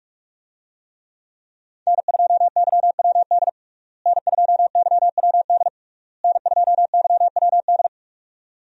Morse code, heard as a keyed tone, about 41 wpm, 700 Hz. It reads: N2YWD N2YWD N2YWD
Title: USA Call Signs - 5 Characters - Repeated 3x - 45wpm